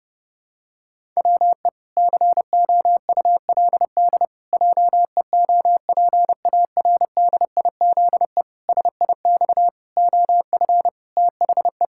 Morse code, heard as a keyed tone, about 28 wpm, 700 Hz. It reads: WE COULD JEOPARDIZE SIX OF THE
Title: Pangram Challenge v2 - 30wpm